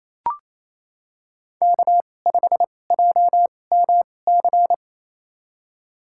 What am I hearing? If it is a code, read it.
Morse code, heard as a keyed tone, about 28 wpm, spaced as for 20 wpm, 700 Hz. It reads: K5JMC
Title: USA Call Signs - 5 Characters - 2x Character Spacing - 28wpm